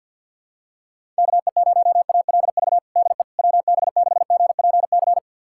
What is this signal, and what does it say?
Morse code, heard as a keyed tone, about 52 wpm, 700 Hz. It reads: KE0ALV DE WB6GPC
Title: USA Callsign DE Callsign - 6 Characters - 50wpm